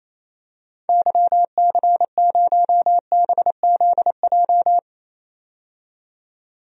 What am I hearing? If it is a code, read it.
Morse code, heard as a keyed tone, about 28 wpm, 700 Hz. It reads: YC0BZJ
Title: DXCC-100 Call Signs - 6 Characters - 28wpm